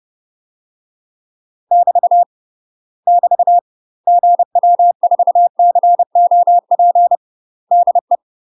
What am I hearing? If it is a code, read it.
Morse code, heard as a keyed tone, about 30 wpm, 700 Hz. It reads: <BT> <BT> GW4COP DE
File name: UK MASTER QSOs TRAINING / 30WPM / UK MASTER QSOs 0050